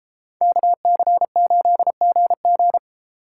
Morse code, heard as a keyed tone, about 31 wpm, 700 Hz. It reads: KC8GG